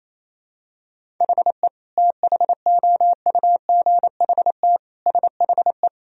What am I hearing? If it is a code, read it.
Morse code, heard as a keyed tone, about 28 wpm, 700 Hz. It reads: HE THOUGHT SHE
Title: Sentences from Top 1000 Words - Review Part 1 - 28wpm